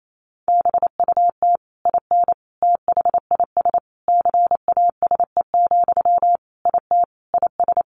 Morse code, heard as a keyed tone, about 28 wpm, 700 Hz. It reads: BUT IN THIS CASE, IT IS
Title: Sentences from Top 1200 Words 28wpm